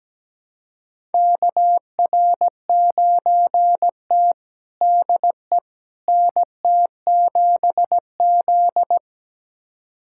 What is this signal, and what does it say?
Morse code, heard as a keyed tone, about 17 wpm, 700 Hz. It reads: KR9T DE NT7Z
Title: USA Callsign DE Callsign - 4 Characters - 17wpm